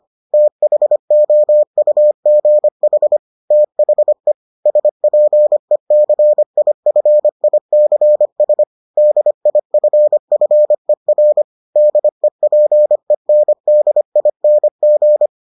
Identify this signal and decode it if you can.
Morse code, heard as a keyed tone, about 25 wpm, 600 Hz. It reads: THOUGH THE SPECIFICS DIFFER DEPENDING